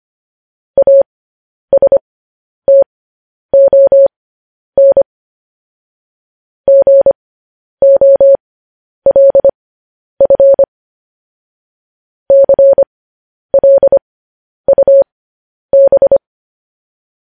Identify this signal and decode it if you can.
Morse code, heard as a keyed tone, about 25 wpm, spaced as for 10 wpm, 550 Hz. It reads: ASTON GOLF CLUB
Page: News Headlines in Morse Code at 10 WPM